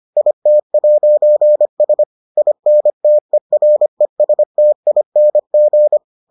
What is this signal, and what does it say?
Morse code, heard as a keyed tone, about 25 wpm, 600 Hz. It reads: IT'S INTERESTING